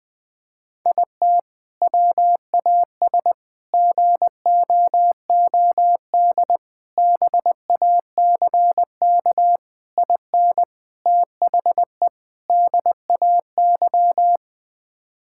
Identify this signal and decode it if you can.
Morse code, heard as a keyed tone, about 21 wpm, 700 Hz. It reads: IT WAS GOOD BACK IN THE DAY
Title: Sentences with 3-Word Phrases - 20wpm